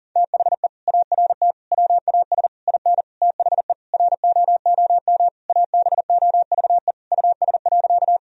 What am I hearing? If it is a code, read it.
Morse code, heard as a keyed tone, about 40 wpm, 700 Hz. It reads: THE ART WAS IN THE ROOM ABOVE US.
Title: Sentences from Top 1000 Words - Review Part 3 - 40wpm